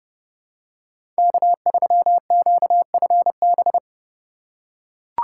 Morse code, heard as a keyed tone, about 30 wpm, 700 Hz. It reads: K3QFB
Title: USA Call Signs - 2x Character Spacing - 30wpm